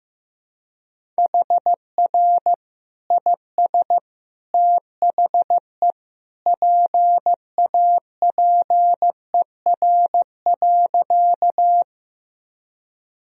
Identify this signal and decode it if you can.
Morse code, heard as a keyed tone, about 15 wpm, 700 Hz. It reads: HR IS THE PAPER.
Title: Sentences from Top 300 Words and Abbreviations 15wpm